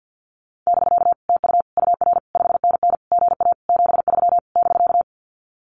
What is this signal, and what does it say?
Morse code, heard as a keyed tone, about 48 wpm, 700 Hz. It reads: <BK> TU UR 5NN GA 73 <BK>
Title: Common QSO Phrases Encore 50wpm